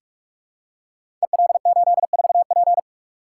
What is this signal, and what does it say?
Morse code, heard as a keyed tone, about 44 wpm, 700 Hz. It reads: EL84P
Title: Code Groups - 5 Letter:Numbers - 45wpm